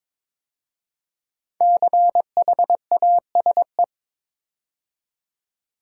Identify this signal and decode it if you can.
Morse code, heard as a keyed tone, about 22 wpm, 700 Hz. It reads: CHASE
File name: 5 Letter Words - First 3 Letters Given - 22wpm